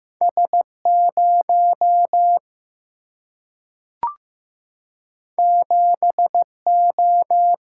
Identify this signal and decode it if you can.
Morse code, heard as a keyed tone, about 15 wpm, 700 Hz. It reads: S 0 7 O
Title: Single-Letter Single-Number 15wpm